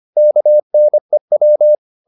Morse code, heard as a keyed tone, about 25 wpm, 600 Hz. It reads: KNEW
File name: the old man and the sea part 1 25wpm